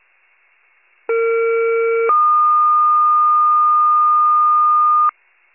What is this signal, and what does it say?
Two-tone page, heard as a one-second tone, then a three-second tone, about 460 then 1220 Hz. It is heard from 1.1 to 5.1 s.